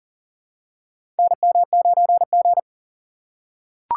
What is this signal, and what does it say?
Morse code, heard as a keyed tone, about 40 wpm, 700 Hz. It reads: NM9G
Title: USA Call Signs - 4 Characters - 2x Character Spacing - 40wpm